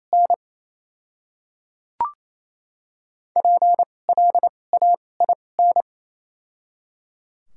Morse code, heard as a keyed tone, about 28 wpm, spaced as for 9 wpm, 700 Hz. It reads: NPLAIN
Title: 2x Character Spacing - Top 500 Words - 28wpm